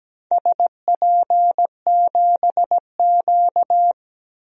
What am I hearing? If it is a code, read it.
Morse code, heard as a keyed tone, about 17 wpm, 700 Hz. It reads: SP7Q
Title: DXCC-100 Call Signs - 4 Characters - 17wpm